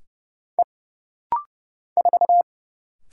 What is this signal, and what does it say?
Morse code, heard as a keyed tone, about 30 wpm, 700 Hz. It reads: E 4
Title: Single Letter-Number - Rapid-Fire 30wpm